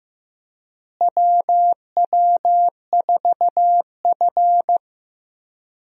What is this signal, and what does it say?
Morse code, heard as a keyed tone, about 15 wpm, 700 Hz. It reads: WW4F
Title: USA Call Signs - 4 Characters - Repeated 3x - 15wpm